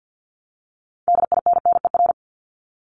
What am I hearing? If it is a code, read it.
Morse code, heard as a keyed tone, about 50 wpm, 700 Hz. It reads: DINNER
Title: Multi-letter Phonemes Repeated 3x 50wpm